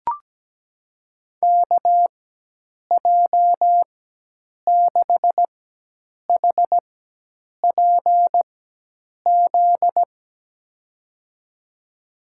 Morse code, heard as a keyed tone, about 17 wpm, spaced as for 8 wpm, 700 Hz. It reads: KJ6HPZ
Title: USA Call Signs - 6 Characters - 4x Character Spacing - 17wpm